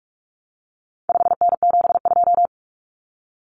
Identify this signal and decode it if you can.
Morse code, heard as a keyed tone, about 44 wpm, 700 Hz. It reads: 5N72
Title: Common Cut Numbers - 4-Digit 45wpm